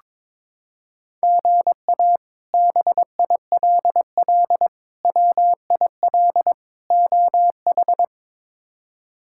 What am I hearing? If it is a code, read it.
Morse code, heard as a keyed tone, about 22 wpm, 700 Hz. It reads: GA BILL WIL OH
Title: K1USN SST QSO Practice - 22wpm